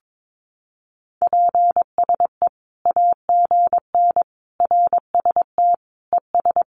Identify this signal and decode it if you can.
Morse code, heard as a keyed tone, about 22 wpm, 700 Hz. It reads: PSE AGN RST ES
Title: Common QSO Phrases 22wpm